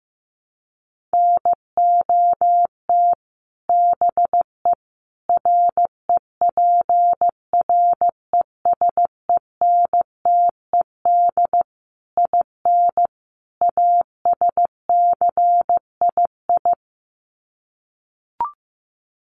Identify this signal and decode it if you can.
Morse code, heard as a keyed tone, about 15 wpm, 700 Hz. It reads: NOT BE REPRESENTED IN ASCII
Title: Sets of 5 Words 15wpm